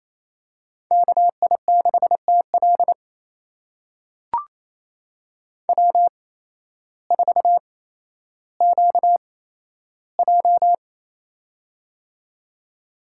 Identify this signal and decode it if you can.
Morse code, heard as a keyed tone, about 28 wpm, spaced as for 13 wpm, 700 Hz. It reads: KI6TL W4QJ
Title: USA Call Signs - 8x Character Spacing - 28wpm